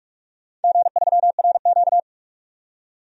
Morse code, heard as a keyed tone, about 47 wpm, 700 Hz. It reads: G3RQ
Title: DXCC-100 Call Signs - 4 Characters - 45wpm